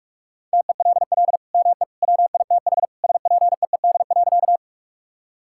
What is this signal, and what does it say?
Morse code, heard as a keyed tone, about 41 wpm, 700 Hz. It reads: TELL ME WITH SPEED.
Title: Sentences from Top 1000 Words - Review Part 5 - 45wpm